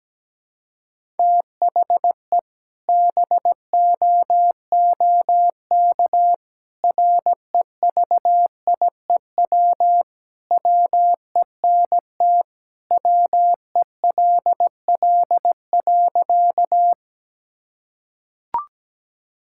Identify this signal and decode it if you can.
Morse code, heard as a keyed tone, about 17 wpm, 700 Hz. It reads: THE BOOK REVIEW WENT WELL.
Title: Sentences from Top 1000 Words - Review Part 6 - 17wpm